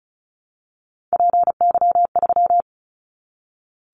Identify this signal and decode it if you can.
Morse code, heard as a keyed tone, about 34 wpm, 700 Hz. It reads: PY3
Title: Callsign Prefixes for Top 100 DXCC - 35wpm